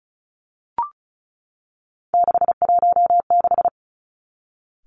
Morse code, heard as a keyed tone, about 36 wpm, 700 Hz. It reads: 616
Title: Three Challenging Letter-Number Combinations 35wpm